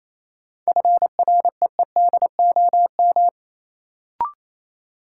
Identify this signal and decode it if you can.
Morse code, heard as a keyed tone, about 28 wpm, 700 Hz. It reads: FREEDOM
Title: Sentences from Top 800 Words - New Word Drill - 28wpm